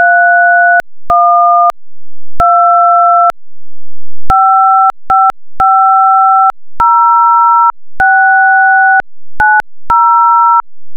DTMF, heard as telephone touch tones, tones of about 770 ms. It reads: 31 2 5550690